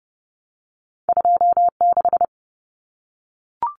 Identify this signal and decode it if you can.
Morse code, heard as a keyed tone, about 30 wpm, 700 Hz. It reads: 26
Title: Common Cut Numbers - 2-Digit 30wpm